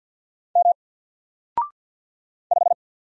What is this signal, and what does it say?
Morse code, heard as a keyed tone, about 49 wpm, 700 Hz. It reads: M 5
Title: Single Character (Letter:Number:Punctuation:Prosign) 50wpm